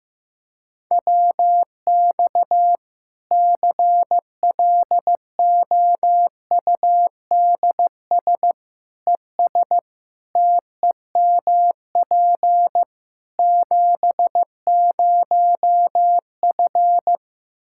Morse code, heard as a keyed tone, about 15 wpm, 700 Hz. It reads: WX CLOUDS ES TEMP 70F
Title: Practice QSO - Exchange 2 - Info - 15wpm